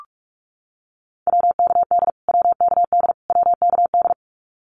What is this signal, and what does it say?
Morse code, heard as a keyed tone, about 44 wpm, 700 Hz. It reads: WKD WKD WKD
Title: Abbreviations Repeated 3x - 45wpm